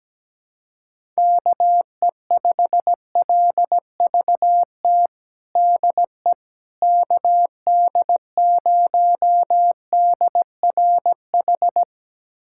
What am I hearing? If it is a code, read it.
Morse code, heard as a keyed tone, about 17 wpm, 700 Hz. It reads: KE5LVT DE KD0DRH
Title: USA Callsign DE Callsign - 6 Characters - 17wpm